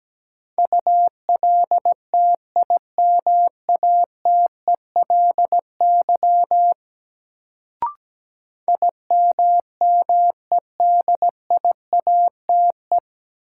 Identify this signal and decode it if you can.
Morse code, heard as a keyed tone, about 17 wpm, 700 Hz. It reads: ULTIMATELY IMMEDIATE
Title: Sentences from Top 1200 Words - New Word Drill - 17wpm